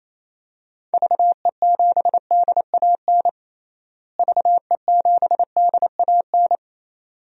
Morse code, heard as a keyed tone, about 28 wpm, 700 Hz. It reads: VE7DAN VE7DAN
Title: DXCC-100 Call Signs - 6 Characters - Repeated 2x - 28wpm